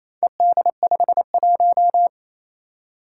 Morse code, heard as a keyed tone, about 28 wpm, 700 Hz. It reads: ED51
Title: Four Challenging Letter-Number Combinations 28wpm